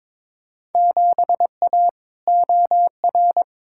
Morse code, heard as a keyed tone, about 21 wpm, 700 Hz. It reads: 7A OR
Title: Field Day Exchanges Encore - 22wpm